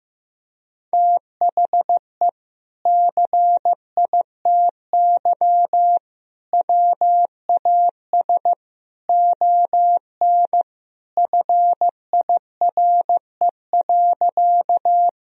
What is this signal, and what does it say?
Morse code, heard as a keyed tone, about 16 wpm, 700 Hz. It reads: THE CITY WAS ON FIRE.
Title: Sentences from Top 1000 Words - Review Part 1 - 15wpm